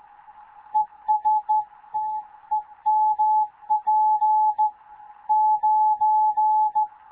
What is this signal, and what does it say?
Morse code, heard as a keyed tone, about 16 wpm, 860 Hz. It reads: ES TEMP 9
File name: Morze-HA2MN / hangp / 2 11 GYAKORLAT